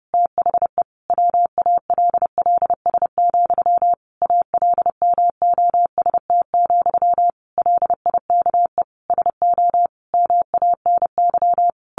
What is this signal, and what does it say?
Morse code, heard as a keyed tone, about 30 wpm, 700 Hz. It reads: THE WALLS, ALMOST, LIKE SO MANY
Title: A Sound Of Thunder 30wpm